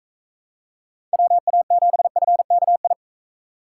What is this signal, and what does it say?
Morse code, heard as a keyed tone, about 42 wpm, 700 Hz. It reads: WA7FKI